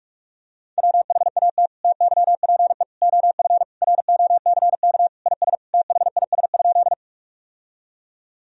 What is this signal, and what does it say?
Morse code, heard as a keyed tone, about 46 wpm, 700 Hz. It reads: WHAT TYPE OF ROCK IS THIS?
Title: Sentences from Top 1000 Words - Review Part 3 - 45wpm